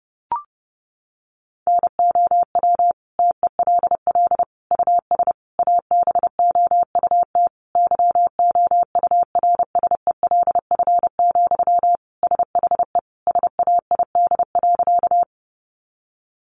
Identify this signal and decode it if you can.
Morse code, heard as a keyed tone, about 30 wpm, 700 Hz. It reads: NOW TELL US ABOUT YOURSELF, SHE SAID.
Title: Sentences from Top 600 Words 30wpm